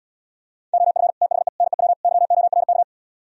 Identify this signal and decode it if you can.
Morse code, heard as a keyed tone, about 77 wpm, 700 Hz. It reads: OF THE NEW YORK